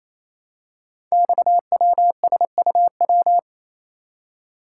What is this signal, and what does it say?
Morse code, heard as a keyed tone, about 27 wpm, 700 Hz. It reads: XWSUW
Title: Code Groups - 5 Letters - 28wpm